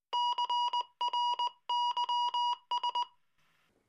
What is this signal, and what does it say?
Morse code, heard as a keyed tone, about 18 wpm, 940 Hz. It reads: CRYS